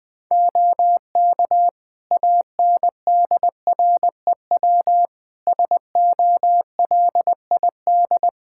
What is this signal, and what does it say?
Morse code, heard as a keyed tone, about 21 wpm, 700 Hz. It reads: OK ANDREW SOLID
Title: Practice QSO - Exchange 3 - More Info - 20wpm